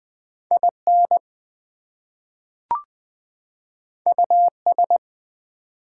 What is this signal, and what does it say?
Morse code, heard as a keyed tone, about 20 wpm, 700 Hz. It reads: IN US
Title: Two Letter Words - 20wpm